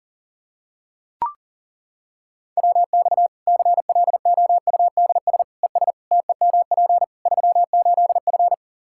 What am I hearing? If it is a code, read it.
Morse code, heard as a keyed tone, about 40 wpm, 700 Hz. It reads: WX CLOUDS ES TEMP 38F
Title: Practice QSO - Exchange 2 - Info - 40wpm